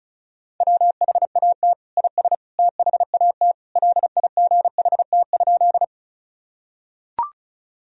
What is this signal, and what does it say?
Morse code, heard as a keyed tone, about 36 wpm, 700 Hz. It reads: WHAT IS THAT LIGHT?
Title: Sentences from Top 1000 Words - Review Part 1 - 35wpm